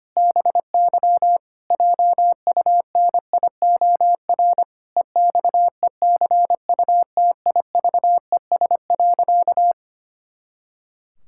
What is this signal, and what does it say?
Morse code, heard as a keyed tone, about 25 wpm, 700 Hz. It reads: BY JUNIOR EXECUTIVES.
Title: Pangram Challenge v2 - 25wpm